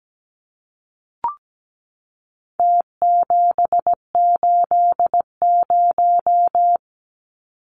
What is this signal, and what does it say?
Morse code, heard as a keyed tone, about 17 wpm, 700 Hz. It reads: T780
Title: Numbers - Contest Style - 4-Digit 17wpm